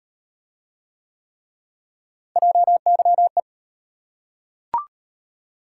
Morse code, heard as a keyed tone, about 36 wpm, 700 Hz. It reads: JYE